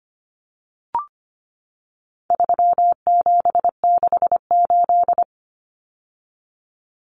Morse code, heard as a keyed tone, about 25 wpm, 700 Hz. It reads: 3768
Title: Numbers - 4-Digit 25wpm